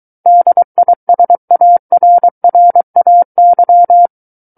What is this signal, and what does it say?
Morse code, heard as a keyed tone, about 23 wpm, 700 Hz. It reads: DISARRAY
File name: Book One / 14-Chapter 14 - 23 WPM